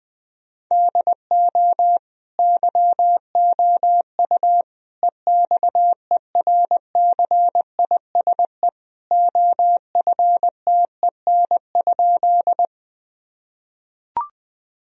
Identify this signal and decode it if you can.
Morse code, heard as a keyed tone, about 19 wpm, 700 Hz. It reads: DO YOU EXERCISE OFTEN?
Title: Sentences from Top 1000 Words - Review Part 4 - 20wpm